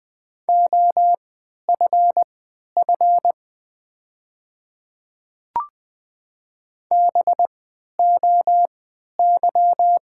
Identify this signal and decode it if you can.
Morse code, heard as a keyed tone, about 20 wpm, spaced as for 11 wpm, 690 Hz. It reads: OFF BOY
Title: 3x Character Spacing - Top 200 Words - 20wpm